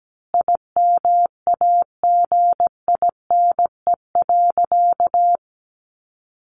Morse code, heard as a keyed tone, about 17 wpm, 700 Hz. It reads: IMAGINE.
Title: Sentences from Top 1400 Words 17wpm